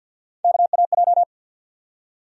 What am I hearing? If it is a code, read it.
Morse code, heard as a keyed tone, about 46 wpm, 700 Hz. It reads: KAJ
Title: USA Call Sign Postfixes - 3 Letters 50wpm